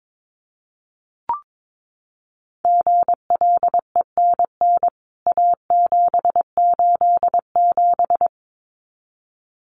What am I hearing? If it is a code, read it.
Morse code, heard as a keyed tone, about 22 wpm, 700 Hz. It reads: GLENN A787
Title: CWT Exchange 22wpm